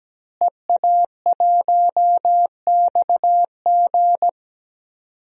Morse code, heard as a keyed tone, about 17 wpm, 700 Hz. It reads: EA1XG
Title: DXCC-100 Call Signs - 5 Characters - Repeated 2x - 17wpm